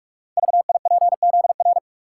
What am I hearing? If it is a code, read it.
Morse code, heard as a keyed tone, about 44 wpm, 700 Hz. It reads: VIPZR
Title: Code Groups - 5 Letters - 45wpm